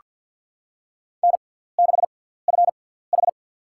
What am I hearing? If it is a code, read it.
Morse code, heard as a keyed tone, about 47 wpm, spaced as for 17 wpm, 690 Hz. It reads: N6FH
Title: USA Call Signs - 4 Characters - 6x Character Spacing - 50wpm